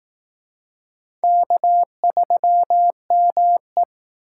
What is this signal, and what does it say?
Morse code, heard as a keyed tone, about 19 wpm, 700 Hz. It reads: K3ME